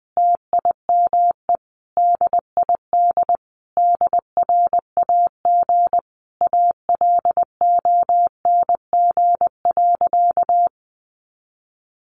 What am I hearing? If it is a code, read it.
Morse code, heard as a keyed tone, about 20 wpm, 700 Hz. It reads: TIME DID DRAG ALONG.
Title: Sentences from Top 1500 Words 20wpm